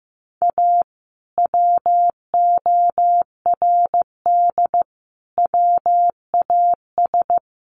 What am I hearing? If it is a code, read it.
Morse code, heard as a keyed tone, about 15 wpm, 700 Hz. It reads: A WORD WAS
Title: Sentences from Top 1400 Words 15wpm